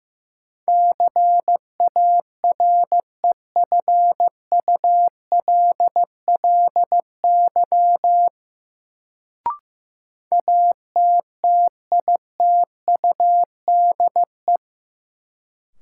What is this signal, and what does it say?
Morse code, heard as a keyed tone, about 15 wpm, 700 Hz. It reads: CAREFULLY ATTITUDE
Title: Sentences from Top 1300 Words - New Word Drill - 15wpm